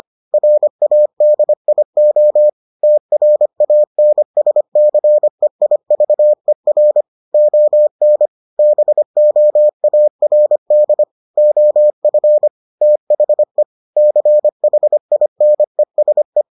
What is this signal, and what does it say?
Morse code, heard as a keyed tone, about 25 wpm, 600 Hz. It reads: RADIO TRANSCEIVER ON BOARD OF THE CHINESE